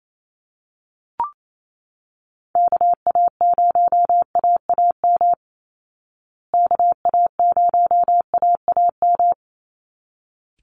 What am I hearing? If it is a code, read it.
Morse code, heard as a keyed tone, about 28 wpm, 700 Hz. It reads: KA0AAM KA0AAM
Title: Call Signs Repeated 3x - 28wpm